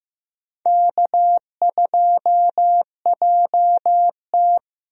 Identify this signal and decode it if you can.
Morse code, heard as a keyed tone, about 15 wpm, 700 Hz. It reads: K2JT
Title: USA Call Signs - 4 Characters - Repeated 3x - 15wpm